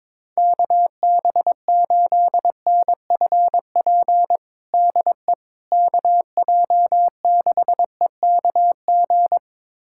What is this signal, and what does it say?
Morse code, heard as a keyed tone, about 22 wpm, 700 Hz. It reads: KB8NFP DE KJ6EKG
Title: USA Callsign DE Callsign - 6 Characters - 22wpm